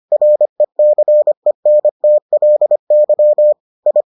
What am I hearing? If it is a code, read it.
Morse code, heard as a keyed tone, about 25 wpm, 600 Hz. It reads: RECENTLY I